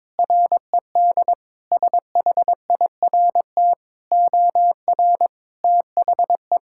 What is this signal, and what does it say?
Morse code, heard as a keyed tone, about 21 wpm, 700 Hz. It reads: RED SHIRT OR THE
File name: practice / 22wpm / Sentences with 2-Word Phrases - 22wpm